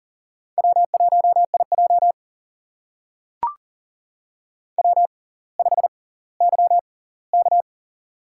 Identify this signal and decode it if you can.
Morse code, heard as a keyed tone, about 40 wpm, spaced as for 20 wpm, 700 Hz. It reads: W1IJ W5YK
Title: USA Call Signs - 6x Character Spacing - 40wpm